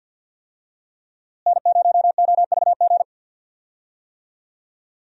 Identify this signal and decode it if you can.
Morse code, heard as a keyed tone, about 52 wpm, 700 Hz. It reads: N0OVG
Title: USA Call Signs - 5 Characters - Repeated 3x - 50wpm